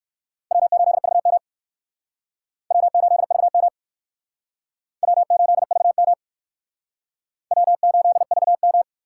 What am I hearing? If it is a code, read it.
Morse code, heard as a keyed tone, about 57 wpm, 700 Hz. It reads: W8VM W8VM W8VM W8VM